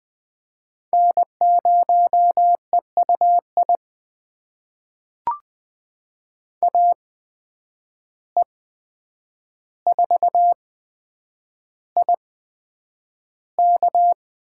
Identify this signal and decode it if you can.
Morse code, heard as a keyed tone, about 20 wpm, spaced as for 9 wpm, 700 Hz. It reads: N0EUI AE4IK
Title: USA Call Signs - 5 Characters - 8x Character Spacing - 20wpm